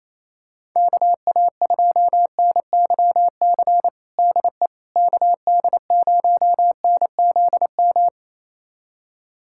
Morse code, heard as a keyed tone, about 28 wpm, 700 Hz. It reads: KA2NYC DE KD0NZM
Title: USA Callsign DE Callsign - 6 Characters - 28wpm